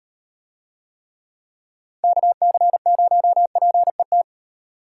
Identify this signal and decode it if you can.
Morse code, heard as a keyed tone, about 38 wpm, 700 Hz. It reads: KC0PET